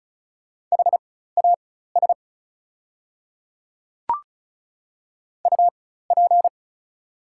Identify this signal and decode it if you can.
Morse code, heard as a keyed tone, about 36 wpm, spaced as for 16 wpm, 690 Hz. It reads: HAS UP
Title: 4x Character Spacing - Top 100 Words - 35wpm